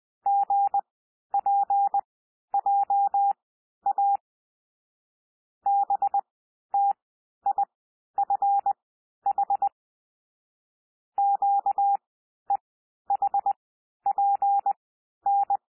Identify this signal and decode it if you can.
Morse code, heard as a keyed tone, about 20 wpm, spaced as for 11 wpm, 800 Hz. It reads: GPJA BTIFH QEHPN
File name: cwmp3 / L3 SOUND LETTERS 20 wpm